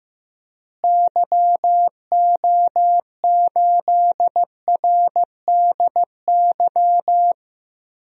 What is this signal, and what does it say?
Morse code, heard as a keyed tone, about 15 wpm, 700 Hz. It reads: YO8RDY
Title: DXCC-100 Call Signs - 6 Characters - Repeated 3x - 15wpm